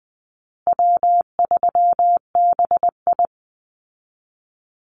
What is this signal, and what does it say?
Morse code, heard as a keyed tone, about 20 wpm, 700 Hz. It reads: W3BI
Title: Easy Call Signs Repeated 3x - 20wpm